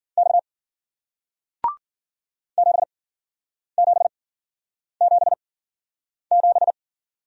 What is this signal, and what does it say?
Morse code, heard as a keyed tone, about 53 wpm, spaced as for 9 wpm, 700 Hz. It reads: X 7777